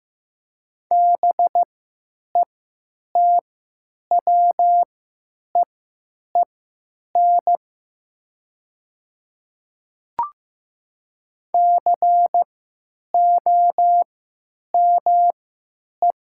Morse code, heard as a keyed tone, about 15 wpm, spaced as for 9 wpm, 700 Hz. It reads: BETWEEN COME
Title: 3x Character Spacing - Top 500 Words - 15wpm